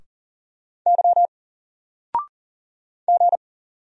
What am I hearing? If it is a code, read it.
Morse code, heard as a keyed tone, about 40 wpm, 700 Hz. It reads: Y G
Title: Single Character (Letter:Number:Punctuation:Prosign) Alt 40wpm